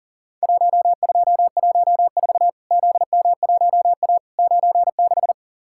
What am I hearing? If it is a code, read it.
Morse code, heard as a keyed tone, about 40 wpm, 700 Hz. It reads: 1214 ZM1A 96
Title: Worked All Europe DX Contest - QTC Exchange - 40wpm